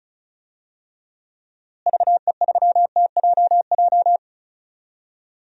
Morse code, heard as a keyed tone, about 34 wpm, 700 Hz. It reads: VE3TJJ
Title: DXCC-100 Call Signs - 6 Characters - Repeated 3x - 35wpm